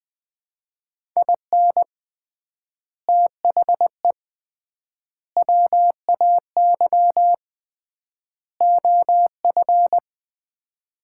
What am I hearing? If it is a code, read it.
Morse code, heard as a keyed tone, about 20 wpm, 700 Hz. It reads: IN THE WAY OF
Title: 3x Word-Spacing - Sets of 4 Words - 20wpm